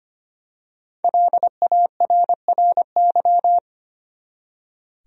Morse code, heard as a keyed tone, about 25 wpm, 700 Hz. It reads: LARRY
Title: USA Common Names - 25wpm